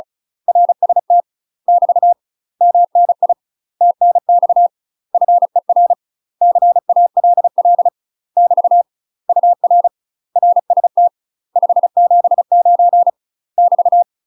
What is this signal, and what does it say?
Morse code, heard as a keyed tone, about 34 wpm, 700 Hz. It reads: RST <BT> MNI TNX FER CALL <BT> UR RST 579 <BT>
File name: UK MASTER QSOs TRAINING / 35WPM / UK MASTER QSOs 0037